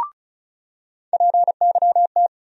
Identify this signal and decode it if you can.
Morse code, heard as a keyed tone, about 36 wpm, 700 Hz. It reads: PYT
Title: USA Call Sign Postfixes - 3 Letters 35wpm